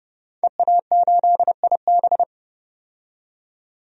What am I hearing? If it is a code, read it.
Morse code, heard as a keyed tone, about 32 wpm, 700 Hz. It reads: EA8IB
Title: DXCC-100 Call Signs - 5 Characters - 30wpm